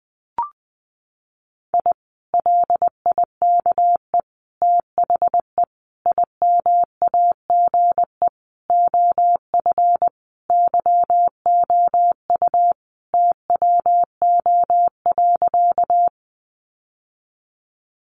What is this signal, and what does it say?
Morse code, heard as a keyed tone, about 19 wpm, 700 Hz. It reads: I LIKE THE IMAGE OF YOU TWO.
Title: Sentences from Top 700 Words 20wpm